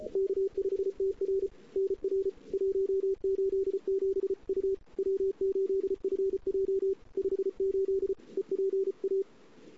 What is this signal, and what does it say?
Morse code, heard as a keyed tone, about 32 wpm, 380 Hz. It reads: K5TR NR 187 U W8FJ 58 EPA
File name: CW / 2008-ss-cw-k5tr-op-n5rz -03:00